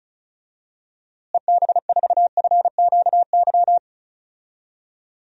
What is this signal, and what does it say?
Morse code, heard as a keyed tone, about 36 wpm, 700 Hz. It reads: EB4FQY
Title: DXCC-100 Call Signs - 6 Characters - Repeated 3x - 35wpm